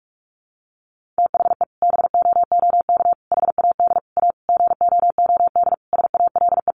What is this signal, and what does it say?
Morse code, heard as a keyed tone, about 44 wpm, 700 Hz. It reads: THE BOOK HAD A GOOD SALE
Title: Sentences from Top 1400 Words 45wpm